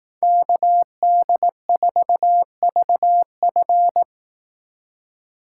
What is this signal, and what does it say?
Morse code, heard as a keyed tone, about 18 wpm, 700 Hz. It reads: KD4VF
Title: Speed Racing - USA 2x2 Call Signs 15wpm